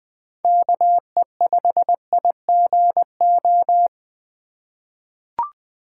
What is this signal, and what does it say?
Morse code, heard as a keyed tone, about 20 wpm, 700 Hz. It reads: KE5IGO
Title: USA Call Signs - 2x Character Spacing - 20wpm